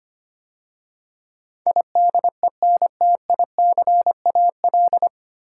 Morse code, heard as a keyed tone, about 25 wpm, 700 Hz. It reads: IDENTICAL
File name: practice / 25wpm / 9 Letter Words - First 3 Letters Given - 25wpm